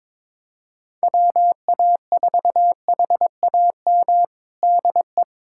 Morse code, heard as a keyed tone, about 22 wpm, 700 Hz. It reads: WA4HAM DE
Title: USA Callsign DE Callsign - 6 Characters - 22wpm